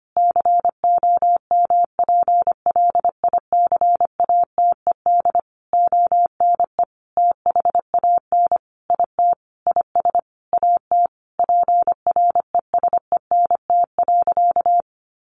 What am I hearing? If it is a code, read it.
Morse code, heard as a keyed tone, about 25 wpm, 700 Hz. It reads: COMPLICATED ONE THAN IT IS AT PRESENT.
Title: Sentences from Top 1200 Words 25wpm